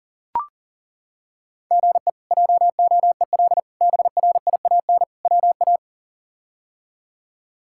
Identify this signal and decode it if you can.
Morse code, heard as a keyed tone, about 40 wpm, 700 Hz. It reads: GE JOEL BRIAN WA
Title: K1USN SST QSO Practice - 40wpm